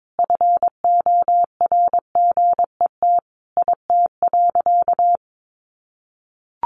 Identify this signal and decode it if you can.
Morse code, heard as a keyed tone, about 22 wpm, 700 Hz. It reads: FORGET IT.
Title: Sentences from Top 1200 Words 22wpm